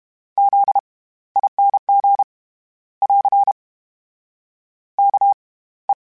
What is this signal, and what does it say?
Morse code, heard as a keyed tone, about 32 wpm, spaced as for 18 wpm, 800 Hz. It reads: ZING <AR> KE
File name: Special INGs / Special INGs-mp3s / SlowToFaster ing @30